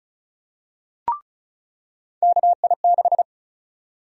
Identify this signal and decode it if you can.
Morse code, heard as a keyed tone, about 36 wpm, 700 Hz. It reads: KI6
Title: USA Call Sign Prefixes - 35wpm